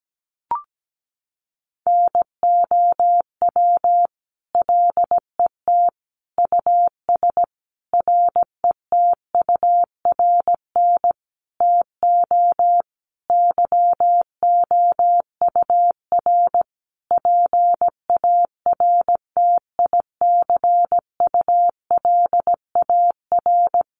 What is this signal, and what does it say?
Morse code, heard as a keyed tone, about 17 wpm, 700 Hz. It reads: NOW LET US RETURN TO YOUR PARTICULAR
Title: Sentences from Top 800 Words 17wpm